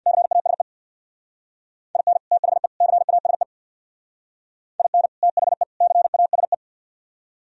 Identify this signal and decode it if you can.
Morse code, heard as a keyed tone, about 59 wpm, 700 Hz. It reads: CASE IN THE CASE IN THE CASE
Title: Speed-Racing - Sets of 3 Words 50wpm